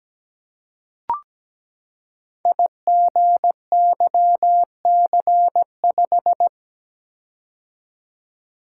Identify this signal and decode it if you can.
Morse code, heard as a keyed tone, about 17 wpm, 700 Hz. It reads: IGYC5
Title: Code Groups - 5 Letter:Numbers - 17wpm